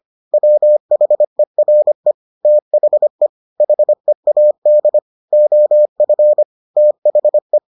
Morse code, heard as a keyed tone, about 25 wpm, 600 Hz. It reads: WHERE THE HEAD OF THE